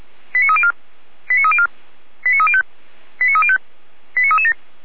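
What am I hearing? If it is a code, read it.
Five-tone selective call (ZVEI), heard as quick runs of five tones, tones of about 70 ms. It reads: 89304 89304 89305 89305 89337
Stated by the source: ZVEI five-tone selective call received on an SDR, channel 455, by Feuerwehr Orlamünde.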